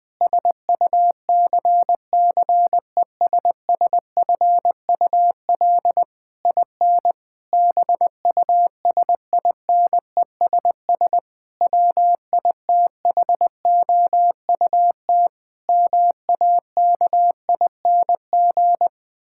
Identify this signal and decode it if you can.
Morse code, heard as a keyed tone, about 20 wpm, 700 Hz. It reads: SUCCESSFUL IN BUSINESS WITHOUT MAKING
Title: Sentences from Top 1000 Words - Review Part 6 - 20wpm